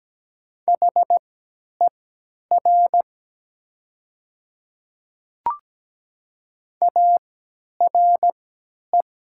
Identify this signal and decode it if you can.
Morse code, heard as a keyed tone, about 17 wpm, spaced as for 10 wpm, 700 Hz. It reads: HER ARE
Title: 3x Character Spacing - Top 100 Words - 17wpm